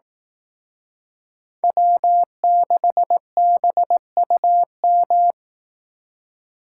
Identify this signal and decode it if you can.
Morse code, heard as a keyed tone, about 18 wpm, 700 Hz. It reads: W6BUM
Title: Speed Racing - USA Call Signs 15wpm